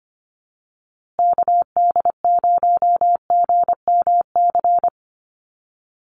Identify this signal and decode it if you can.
Morse code, heard as a keyed tone, about 25 wpm, 700 Hz. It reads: KD0GMC
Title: USA Call Signs Repeated 3x 25wpm